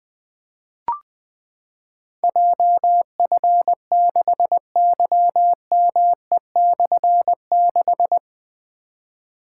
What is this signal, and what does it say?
Morse code, heard as a keyed tone, about 20 wpm, 700 Hz. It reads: JF6YME/6
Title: Difficult Call Signs 20wpm